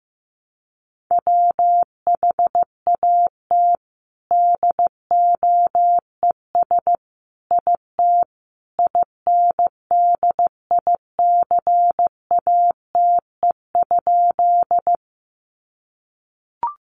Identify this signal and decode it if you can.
Morse code, heard as a keyed tone, about 15 wpm, 700 Hz. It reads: WHAT DOES IT INDICATE?
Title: Sentences from Top 1300 Words 15wpm